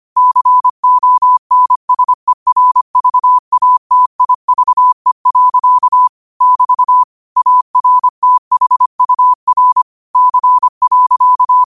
Morse code, heard as a keyed tone, about 25 wpm, 1000 Hz. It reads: CONSERVATIVE. <BT> ARTHUR C.